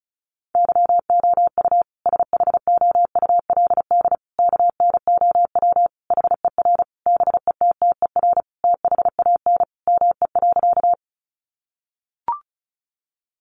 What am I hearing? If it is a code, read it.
Morse code, heard as a keyed tone, about 34 wpm, 700 Hz. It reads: YOU SHOULD KNOW HER BETTER THAN ME.
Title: Sentences from Top 400 Words 35wpm